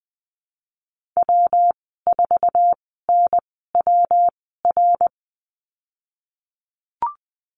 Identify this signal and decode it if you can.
Morse code, heard as a keyed tone, about 20 wpm, spaced as for 14 wpm, 700 Hz. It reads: W4NWR
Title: 128 Lesson 21g - Number 6 - Call Signs 20wpm